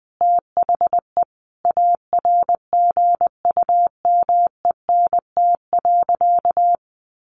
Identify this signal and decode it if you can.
Morse code, heard as a keyed tone, about 20 wpm, 700 Hz. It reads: THE ARGUMENT.
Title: Sentences from Top 1400 Words 20wpm